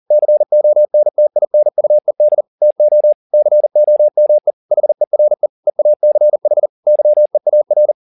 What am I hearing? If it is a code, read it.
Morse code, heard as a keyed tone, about 40 wpm, 600 Hz. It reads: CONTINUED TO COME HERE EACH YEAR